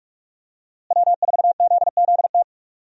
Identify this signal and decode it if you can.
Morse code, heard as a keyed tone, about 46 wpm, 700 Hz. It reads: W4ZZT
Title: USA Call Signs - 5 Characters - Repeated 3x - 45wpm